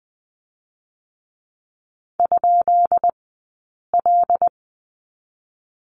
Morse code, heard as a keyed tone, about 20 wpm, 700 Hz. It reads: ? L
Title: 132 Lesson 22d - Question Mark - Two Characters 20wpm